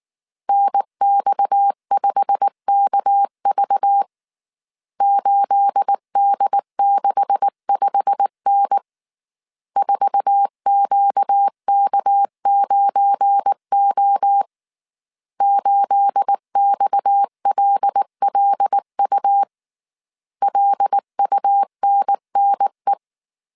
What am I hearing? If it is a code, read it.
Morse code, heard as a keyed tone, about 19 wpm, 800 Hz. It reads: NX5KV 8D65N 4QK9O 8XLLU LUNNE
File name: CW / DET / 1 9 1500C S A 18wpm 2459